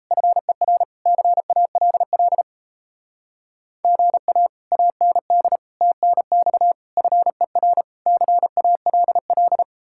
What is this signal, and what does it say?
Morse code, heard as a keyed tone, about 36 wpm, 700 Hz. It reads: FER CALL GA AND TNX FER CALL